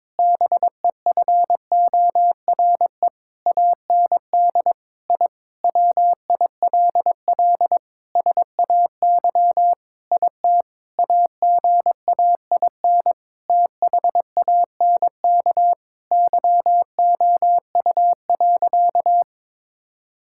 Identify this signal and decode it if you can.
Morse code, heard as a keyed tone, about 22 wpm, 700 Hz. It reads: BEFORE AND I WILL SAY IT AGAIN THANK YOU.
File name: practice / 22wpm / Sentences with 8-Word Phrases - 22wpm